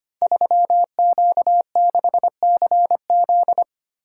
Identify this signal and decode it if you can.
Morse code, heard as a keyed tone, about 25 wpm, 700 Hz. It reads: 3Q6CZ
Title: Code Groups - 5 Letter:Numbers - 25wpm